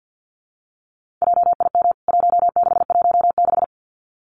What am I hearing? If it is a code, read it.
Morse code, heard as a keyed tone, about 50 wpm, 700 Hz. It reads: JIM 1616
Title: CWT Exchange 50wpm